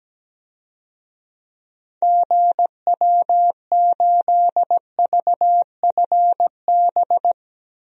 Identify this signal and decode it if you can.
Morse code, heard as a keyed tone, about 17 wpm, 700 Hz. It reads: GW8VFB
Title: DXCC-100 Call Signs - 6 Characters - 17wpm